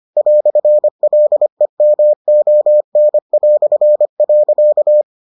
Morse code, heard as a keyed tone, about 25 wpm, 600 Hz. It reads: "LEMON".